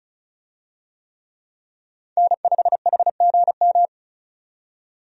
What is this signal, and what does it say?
Morse code, heard as a keyed tone, about 34 wpm, 700 Hz. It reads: N5HGM
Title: USA Call Signs - 5 Characters - 35wpm